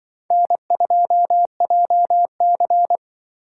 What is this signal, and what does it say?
Morse code, heard as a keyed tone, about 24 wpm, 700 Hz. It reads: N2JC
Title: Speed Racing - USA 1x2 Call Signs 20wpm